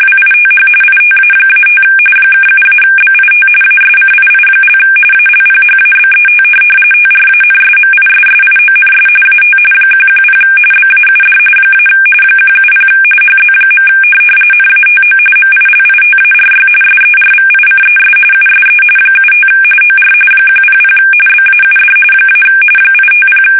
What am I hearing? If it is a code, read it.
RTTY, about 45.45 baud. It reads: WITH ITA-5 (7-BIT ASCII) ALPHABET TRANSMISSIONS AT 110 BAUD - 170 HZ. NATO MILITARY SERVICES USE 75 OR 100 BAUD - 850
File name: fsk850 / teletype-RTTY